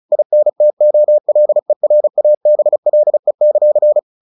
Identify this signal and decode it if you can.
Morse code, heard as a keyed tone, about 34 wpm, 600 Hz. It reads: INTOLERABLE;
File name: the old man and the sea / the old man and the sea 1 35wpm